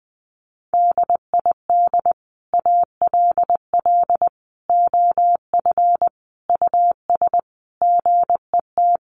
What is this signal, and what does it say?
Morse code, heard as a keyed tone, about 20 wpm, 700 Hz. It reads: DID ALL OF US GET
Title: A Sound Of Thunder 20wpm